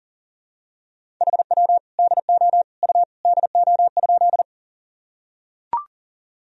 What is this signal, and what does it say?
Morse code, heard as a keyed tone, about 40 wpm, 700 Hz. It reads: HW DO U DO?
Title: Sentences from Top 100 Words and Abbreviations 40wpm